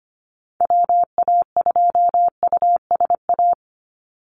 Morse code, heard as a keyed tone, about 25 wpm, 700 Hz. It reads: WA2USA
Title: NA SOTA Activator Callsigns 25wpm